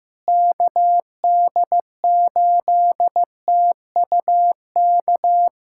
Morse code, heard as a keyed tone, about 15 wpm, 700 Hz. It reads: KD8TUK
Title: USA Callsign DE Callsign 15wpm